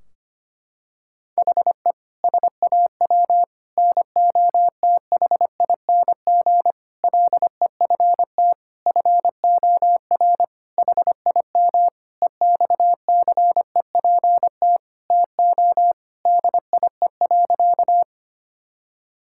Morse code, heard as a keyed tone, about 25 wpm, 700 Hz. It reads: HE SAW NOTHING LEFT FOR HIM EXCEPT TO DIE.
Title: Sentences from Top 1600 Words 25wpm